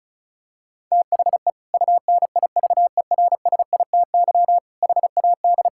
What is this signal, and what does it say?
Morse code, heard as a keyed tone, about 36 wpm, 700 Hz. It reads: THE UNIVERSITY HAD
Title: Sentences from Top 1000 Words - Review Part 5 - 35wpm